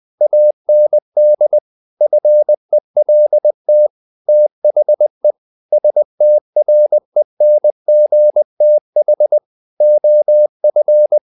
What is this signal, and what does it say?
Morse code, heard as a keyed tone, about 20 wpm, 600 Hz. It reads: AND FELT THE STRENGTH OF
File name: the old man and the sea / the old man and the sea part 1 20wpm